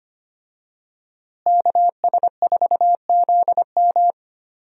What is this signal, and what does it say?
Morse code, heard as a keyed tone, about 25 wpm, 700 Hz. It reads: KS4ZM
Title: USA Call Signs - 5 Characters - Repeated 2x - 25wpm